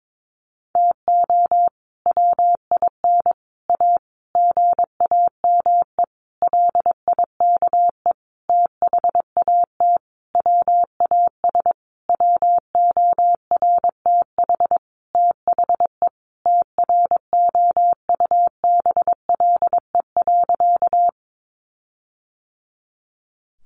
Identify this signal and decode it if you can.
Morse code, heard as a keyed tone, about 22 wpm, 700 Hz. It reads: TO WIN A GAME LIKE THAT WAS WORTH THE TROUBLE.
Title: Sentences from Top 1500 Words 22wpm